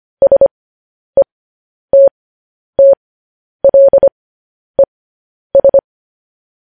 Morse code, heard as a keyed tone, about 25 wpm, spaced as for 10 wpm, 550 Hz. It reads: SETTLES
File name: podcast / News headlines-1x-25x10